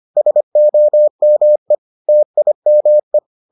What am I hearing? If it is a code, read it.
Morse code, heard as a keyed tone, about 25 wpm, 600 Hz. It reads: SOME TIME